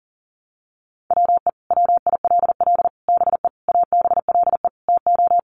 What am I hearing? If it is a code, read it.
Morse code, heard as a keyed tone, about 40 wpm, 700 Hz. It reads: WE WILL BE ABLE TO
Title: Sets of 5 Words Encore 40wpm